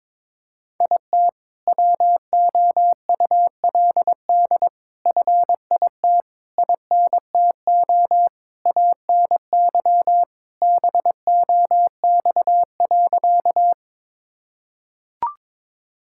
Morse code, heard as a keyed tone, about 22 wpm, 700 Hz. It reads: IT WOULD FIT INTO ANY BOX.
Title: Sentences from Top 1000 Words - Review Part 5 - 22wpm